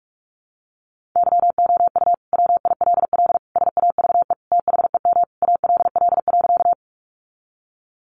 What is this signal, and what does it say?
Morse code, heard as a keyed tone, about 44 wpm, 700 Hz. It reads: YOU WILL SAVE THEM ALL.
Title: Sentences from Top 1100 Words 45wpm